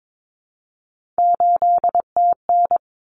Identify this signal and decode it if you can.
Morse code, heard as a keyed tone, about 22 wpm, 700 Hz. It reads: 8TN
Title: Common Cut Numbers - 3-Digit 22wpm